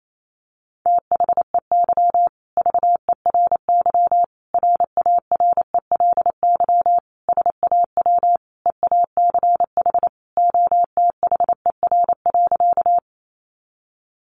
Morse code, heard as a keyed tone, about 28 wpm, 700 Hz. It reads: THEY VERY RARELY SAW EACH OTHER.
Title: Sentences from Top 1400 Words 28wpm